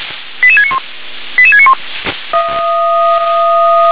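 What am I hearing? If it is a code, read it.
Five-tone selective call (ZVEI), heard as quick runs of five tones, tones of about 70 ms. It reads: 88691 88691
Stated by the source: ZVEI five-tone call 88796 received on an SDR.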